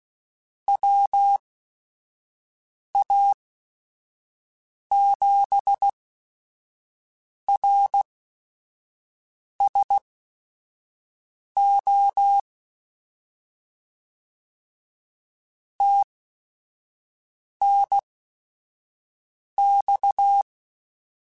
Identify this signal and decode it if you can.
Morse code, heard as a keyed tone, about 16 wpm, spaced as for 5 wpm, 800 Hz. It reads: WA7RSO TNX